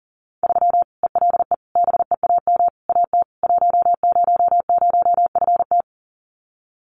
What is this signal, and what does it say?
Morse code, heard as a keyed tone, about 40 wpm, 700 Hz. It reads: 3 ELE BEAM AT 100FT
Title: Common QSO Phrases 40wpm